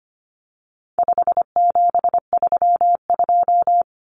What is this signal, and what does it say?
Morse code, heard as a keyed tone, about 25 wpm, 700 Hz. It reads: 5732
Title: Numbers - 4-Digit 25wpm